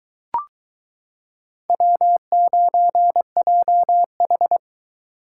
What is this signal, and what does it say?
Morse code, heard as a keyed tone, about 22 wpm, 700 Hz. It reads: W9JH